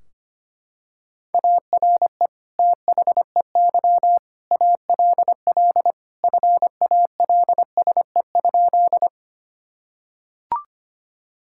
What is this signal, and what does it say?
Morse code, heard as a keyed tone, about 25 wpm, 700 Hz. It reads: ARE THEY ALL FALSE?
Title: Sentences from Top 1600 Words 25wpm